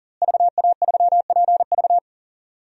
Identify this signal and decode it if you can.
Morse code, heard as a keyed tone, about 37 wpm, 700 Hz. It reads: VA3PV
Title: DXCC-100 Call Signs - 5 Characters - Repeated 3x - 40wpm